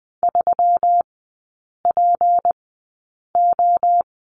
Morse code, heard as a keyed tone, about 20 wpm, spaced as for 8 wpm, 700 Hz. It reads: 3PO
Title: 152 Lesson 25e - Letter p - Three Characters 20wpm